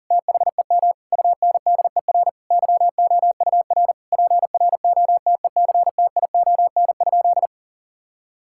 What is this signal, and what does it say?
Morse code, heard as a keyed tone, about 40 wpm, 700 Hz. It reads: THEM UNDER YOUR PROTECTION?
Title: Sentences from Top 1600 Words 40wpm